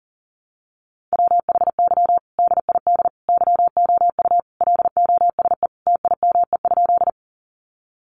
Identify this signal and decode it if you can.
Morse code, heard as a keyed tone, about 40 wpm, 700 Hz. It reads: WHY DID YOU LOSE TIME?
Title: Sentences from Top 1100 Words 40wpm